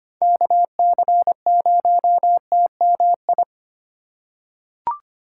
Morse code, heard as a keyed tone, about 25 wpm, 700 Hz. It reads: KC0TMI
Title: USA Call Signs - 6 Characters - 3x Character Spacing - 25wpm